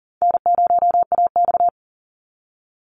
Morse code, heard as a keyed tone, about 40 wpm, 700 Hz. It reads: N0AX
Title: Easy Call Signs Repeated 3x - 40wpm